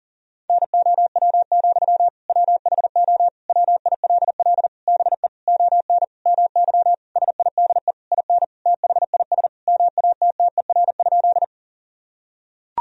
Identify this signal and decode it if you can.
Morse code, heard as a keyed tone, about 37 wpm, 700 Hz. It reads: NOW, WHO WILL BE ON MY SIDE IN THIS MATTER?
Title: Sentences from Top 1000 Words - Review Part 1 - 40wpm